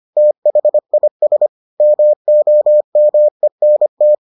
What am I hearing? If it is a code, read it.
Morse code, heard as a keyed tone, about 25 wpm, 600 Hz. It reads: THIS MOMENT